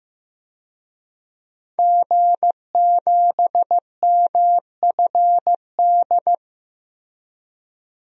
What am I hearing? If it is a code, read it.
Morse code, heard as a keyed tone, about 15 wpm, 700 Hz. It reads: G7MFD
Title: DXCC-100 Call Signs - 5 Characters - Repeated 3x - 15wpm